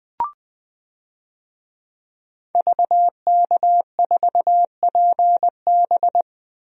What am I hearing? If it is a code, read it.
Morse code, heard as a keyed tone, about 20 wpm, 700 Hz. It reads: VK4PB
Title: DXCC-100 Call Signs - 5 Characters - 20wpm